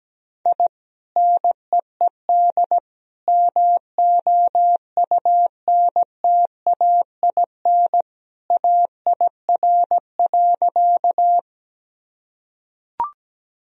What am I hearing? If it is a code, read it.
Morse code, heard as a keyed tone, about 17 wpm, 700 Hz. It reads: I NEED MOUNTAIN AIR.
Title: Sentences from Top 1000 Words - Review Part 1 - 17wpm